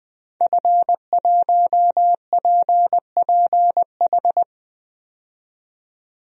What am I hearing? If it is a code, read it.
Morse code, heard as a keyed tone, about 20 wpm, 700 Hz. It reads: F1PPH
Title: DXCC-100 Call Signs - 5 Characters - 20wpm